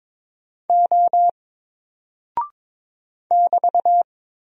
Morse code, heard as a keyed tone, about 23 wpm, 700 Hz. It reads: O <BT>
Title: Single Character (Letter:Number:Punctuation:Prosign) Alt 22wpm